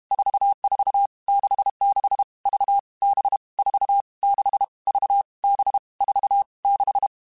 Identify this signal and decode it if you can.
Morse code, heard as a keyed tone, about 32 wpm, 800 Hz. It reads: 44 66 V B 4 6 V B 4 6
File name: Groups@30 / Groups@30 mp3s / VB46@30